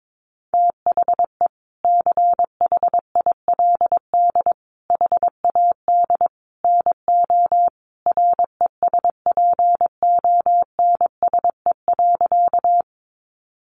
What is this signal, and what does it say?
Morse code, heard as a keyed tone, about 22 wpm, 700 Hz. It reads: THE CHILD HAD NO RESPONSE.
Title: Sentences from Top 1400 Words 22wpm